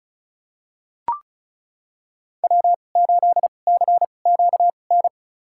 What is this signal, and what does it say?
Morse code, heard as a keyed tone, about 36 wpm, spaced as for 25 wpm, 690 Hz. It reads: W8CQN
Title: USA Call Signs - 5 Characters - 2x Character Spacing - 35wpm